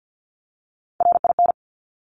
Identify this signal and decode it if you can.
Morse code, heard as a keyed tone, about 50 wpm, 700 Hz. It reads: RIN
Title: Top Three Letter Combinations in Words 50wpm